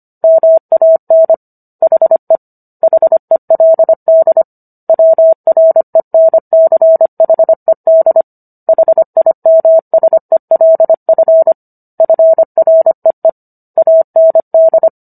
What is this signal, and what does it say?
Morse code, heard as a keyed tone, about 25 wpm, 650 Hz. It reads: MAN HE HELD WRENCHED HIMSELF FREE AND